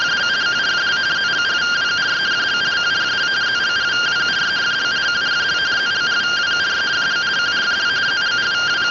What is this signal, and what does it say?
Two-tone teleprinter signal, about 170 Hz shift.